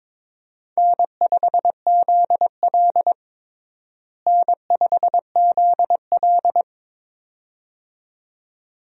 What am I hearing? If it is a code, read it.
Morse code, heard as a keyed tone, about 22 wpm, 700 Hz. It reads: N5ZL N5ZL
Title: USA Call Signs - 4 Characters - Repeated 2x - 22wpm